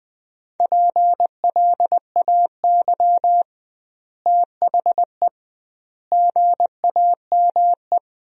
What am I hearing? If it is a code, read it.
Morse code, heard as a keyed tone, about 20 wpm, 700 Hz. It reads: PLAY THE GAME
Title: 2x Word-Spacing - Sentences from Top 500 Words - 20wpm